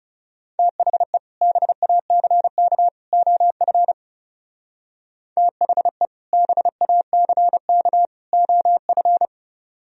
Morse code, heard as a keyed tone, about 36 wpm, 700 Hz. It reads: THE BACK OF THE BACK OF
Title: Speed-Racing - Sets of 3 Words 30wpm